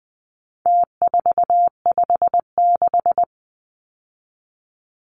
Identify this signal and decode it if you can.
Morse code, heard as a keyed tone, about 20 wpm, 700 Hz. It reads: T456
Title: Numbers - Contest Style - 4-Digit 20wpm